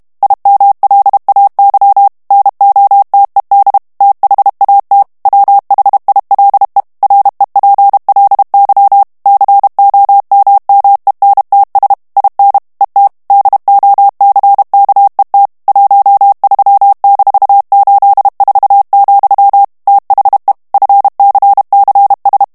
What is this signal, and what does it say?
Morse code, heard as a keyed tone, about 32 wpm, 800 Hz. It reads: IMLAY NOTED THAT WHILE REPLY COMMENTS IN ET DOCKET 13-84, THE FCCS
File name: morse mp3 / ANGLAIS texte clair 3 vitesse 30 mots minute